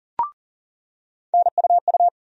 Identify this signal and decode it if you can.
Morse code, heard as a keyed tone, about 40 wpm, 700 Hz. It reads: NUU
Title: USA Call Sign Postfixes - 3 Letters 40wpm